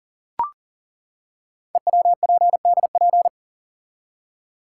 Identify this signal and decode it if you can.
Morse code, heard as a keyed tone, about 40 wpm, 700 Hz. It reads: EWPDP
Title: Code Groups - 5 Letters - 40wpm